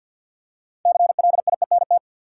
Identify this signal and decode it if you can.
Morse code, heard as a keyed tone, about 46 wpm, 700 Hz. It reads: CLIENT
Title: Six Letter Words - 50wpm